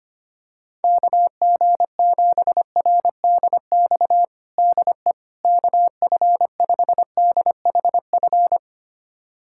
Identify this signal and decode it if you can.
Morse code, heard as a keyed tone, about 25 wpm, 700 Hz. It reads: KG7RDX DE KF5DHF
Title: USA Callsign DE Callsign - 6 Characters - 25wpm